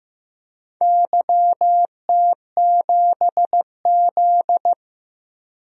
Morse code, heard as a keyed tone, about 15 wpm, 700 Hz. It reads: YT7Z
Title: DXCC-100 Call Signs - 4 Characters - 15wpm